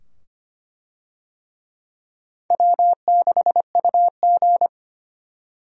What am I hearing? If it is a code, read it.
Morse code, heard as a keyed tone, about 25 wpm, 700 Hz. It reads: W6UG
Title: USA Call Signs - 4 Characters - 25wpm